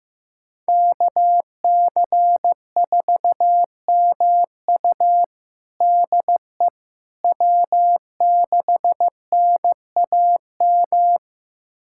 Morse code, heard as a keyed tone, about 15 wpm, 700 Hz. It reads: KC4MU DE W6NAM
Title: USA Callsign DE Callsign - 5 Characters - 15wpm